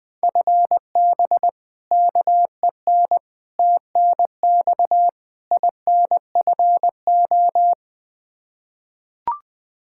Morse code, heard as a keyed tone, about 21 wpm, 700 Hz. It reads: FB KEN TNX INFO
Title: Practice QSO - Exchange 3 - More Info - 20wpm